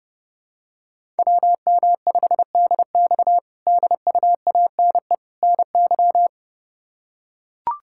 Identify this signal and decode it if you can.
Morse code, heard as a keyed tone, about 32 wpm, 700 Hz. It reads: WM5DX DUANE NY
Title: K1USN SST QSO Practice - 30wpm